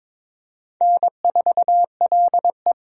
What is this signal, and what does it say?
Morse code, heard as a keyed tone, about 23 wpm, 700 Hz. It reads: N4LE
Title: USA Call Signs - 4 Characters - 22wpm